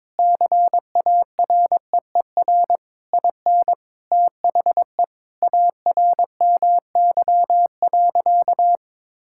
Morse code, heard as a keyed tone, about 22 wpm, 700 Hz. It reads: CAREER IN THE ARMY.
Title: Sentences from Top 1000 Words - Review Part 6 - 22wpm